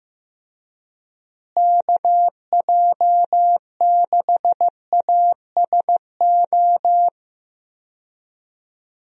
Morse code, heard as a keyed tone, about 15 wpm, 700 Hz. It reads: KJ6ASO
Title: USA Call Signs - 6 Characters - Repeated 3x - 15wpm